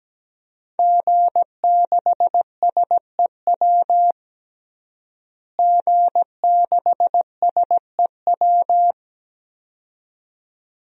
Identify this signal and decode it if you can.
Morse code, heard as a keyed tone, about 17 wpm, 700 Hz. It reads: G6SEW G6SEW
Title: DXCC-100 Call Signs - 5 Characters - Repeated 2x - 17wpm